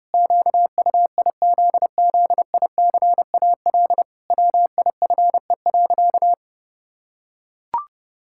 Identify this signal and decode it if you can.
Morse code, heard as a keyed tone, about 30 wpm, 700 Hz. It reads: QUIZZICAL WIFE.
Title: Pangram Challenge v2 - 30wpm